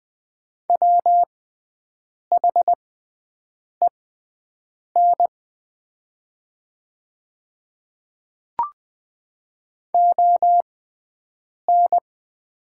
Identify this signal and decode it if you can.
Morse code, heard as a keyed tone, about 20 wpm, spaced as for 7 wpm, 700 Hz. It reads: WHEN ON
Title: 6x Character Spacing - Top 100 Words - 20wpm